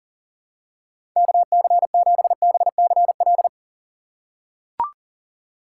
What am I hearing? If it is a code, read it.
Morse code, heard as a keyed tone, about 40 wpm, 700 Hz. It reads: KC7BCL
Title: USA Call Signs - 3x Character Spacing - 40wpm